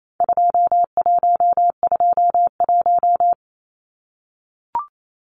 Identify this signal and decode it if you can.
Morse code, heard as a keyed tone, about 28 wpm, 700 Hz. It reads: 2121
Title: Numbers - 4-Digit 28wpm